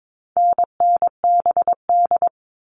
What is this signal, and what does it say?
Morse code, heard as a keyed tone, about 22 wpm, 700 Hz. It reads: NNBD
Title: All Cut Numbers - 4-Digit 22wpm